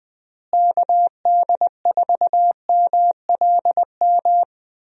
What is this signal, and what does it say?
Morse code, heard as a keyed tone, about 20 wpm, 700 Hz. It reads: KD4MLM
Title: USA Call Signs - 6 Characters - Repeated 2x - 20wpm